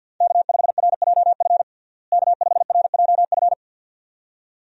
Morse code, heard as a keyed tone, about 49 wpm, 700 Hz. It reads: K5RJF K5RJF
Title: USA Call Signs - 5 Characters - Repeated 2x - 50wpm